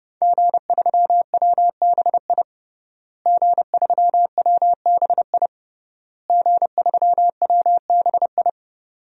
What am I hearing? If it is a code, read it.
Morse code, heard as a keyed tone, about 29 wpm, 700 Hz. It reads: G3WBI G3WBI G3WBI
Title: DXCC-100 Call Signs - 5 Characters - Repeated 3x - 30wpm